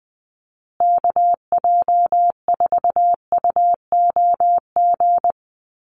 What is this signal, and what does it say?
Morse code, heard as a keyed tone, about 20 wpm, 700 Hz. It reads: KJ4UOG
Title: USA Call Signs Repeated 3x 20wpm